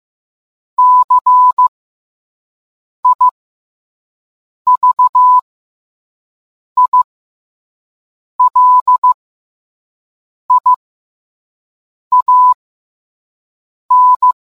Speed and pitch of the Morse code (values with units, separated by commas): 15 wpm, spaced as for 5 wpm, 1000 Hz